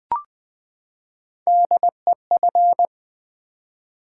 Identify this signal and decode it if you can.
Morse code, heard as a keyed tone, about 20 wpm, 690 Hz. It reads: DEF
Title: Build A Word - 3 Letters At A Time - 6-Letter Words - 20wpm